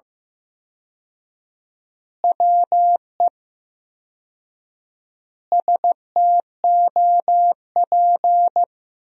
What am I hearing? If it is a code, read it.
Morse code, heard as a keyed tone, about 16 wpm, 700 Hz. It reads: WE STOP
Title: 4x Word-Spacing - Sentences from Top 300 Words - 15wpm